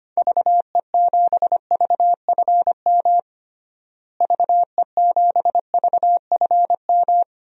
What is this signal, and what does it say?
Morse code, heard as a keyed tone, about 25 wpm, 700 Hz. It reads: VE7VFM VE7VFM
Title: DXCC-100 Call Signs - 6 Characters - Repeated 3x - 25wpm